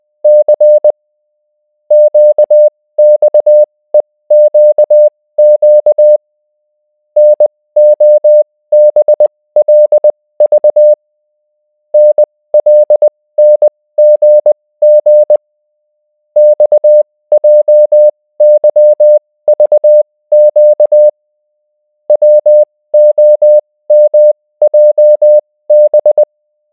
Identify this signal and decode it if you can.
Morse code, heard as a keyed tone, about 20 wpm, spaced as for 16 wpm, 600 Hz. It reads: C QXEQQ NOBLV NLNGG XJYVQ WOMJB